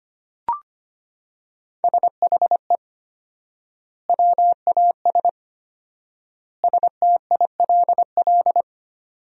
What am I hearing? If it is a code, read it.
Morse code, heard as a keyed tone, about 25 wpm, 700 Hz. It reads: SHE WAS STILL
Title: 4x Word-Spacing - Sentences from Top 500 Words - 25wpm